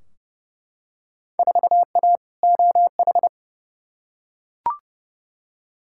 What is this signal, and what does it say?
Morse code, heard as a keyed tone, about 30 wpm, 700 Hz. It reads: 4A OH
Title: Field Day Exchanges - 30wpm